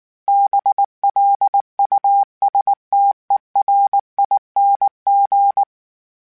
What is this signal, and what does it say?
Morse code, heard as a keyed tone, about 19 wpm, 800 Hz. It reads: BLUSTERING